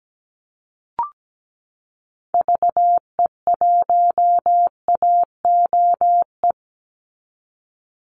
Morse code, heard as a keyed tone, about 17 wpm, 700 Hz. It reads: VE1AOE
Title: Call Signs - 17wpm